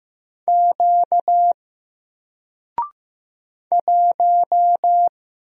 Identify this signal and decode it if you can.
Morse code, heard as a keyed tone, about 15 wpm, 700 Hz. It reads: Q 1
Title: Single Character (Letter:Number:Punctuation:Prosign) Alt 15wpm